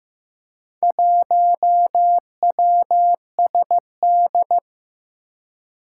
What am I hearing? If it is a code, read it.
Morse code, heard as a keyed tone, about 15 wpm, 700 Hz. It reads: 1WSD
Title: Four Challenging Letter-Number Combinations 15wpm